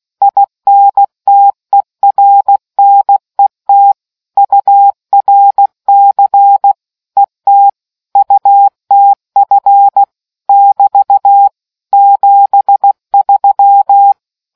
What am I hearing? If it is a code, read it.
Morse code, heard as a keyed tone, about 16 wpm, 800 Hz. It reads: INTERNET URC ET UTF <BT> 73